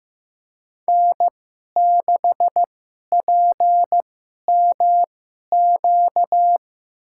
Morse code, heard as a keyed tone, about 15 wpm, spaced as for 11 wpm, 700 Hz. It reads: N6PMQ
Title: USA Call Signs - 5 Characters - 2x Character Spacing - 15wpm